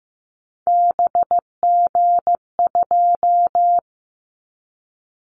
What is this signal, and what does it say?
Morse code, heard as a keyed tone, about 15 wpm, 700 Hz. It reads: BG2
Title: Callsign Prefixes for Top 100 DXCC - 15wpm